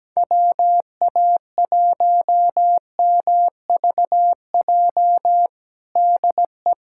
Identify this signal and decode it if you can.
Morse code, heard as a keyed tone, about 17 wpm, 700 Hz. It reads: WA1MVJ DE
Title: USA Callsign DE Callsign - 6 Characters - 17wpm